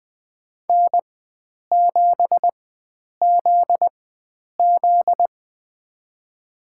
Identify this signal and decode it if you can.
Morse code, heard as a keyed tone, about 20 wpm, spaced as for 9 wpm, 700 Hz. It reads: N7ZZ
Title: USA Call Signs - 4 Characters - 4x Character Spacing - 20wpm